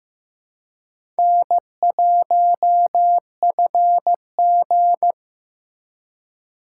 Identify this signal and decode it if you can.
Morse code, heard as a keyed tone, about 15 wpm, 700 Hz. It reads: N1FG
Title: USA Call Signs - 4 Characters - 4x Character Spacing - 15wpm